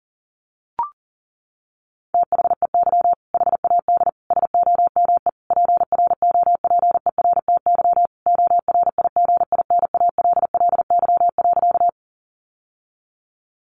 Morse code, heard as a keyed tone, about 40 wpm, 700 Hz. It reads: THEY HAD SOME PROPERTY ORIGINALLY.
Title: Sentences from Top 1500 Words 40wpm